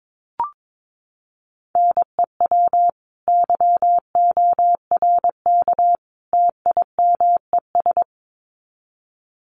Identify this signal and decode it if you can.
Morse code, heard as a keyed tone, about 22 wpm, 700 Hz. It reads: NEW YORK TIMES
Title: Sets of 3 Words 22wpm